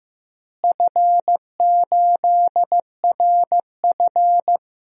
Morse code, heard as a keyed tone, about 15 wpm, 700 Hz. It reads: F8RF
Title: DXCC-100 Call Signs - 4 Characters - Repeated 3x - 15wpm